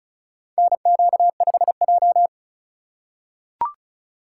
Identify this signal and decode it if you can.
Morse code, heard as a keyed tone, about 36 wpm, 700 Hz. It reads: NQ5J
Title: USA Call Signs - 4 Characters - 3x Character Spacing - 35wpm